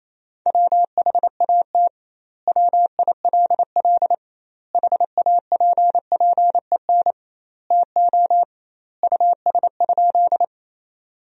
Morse code, heard as a keyed tone, about 28 wpm, 700 Hz. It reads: WHAT WILL HAPPEN TO US?
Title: 2x Word-Spacing - Sentences from Top 400 Words - 28wpm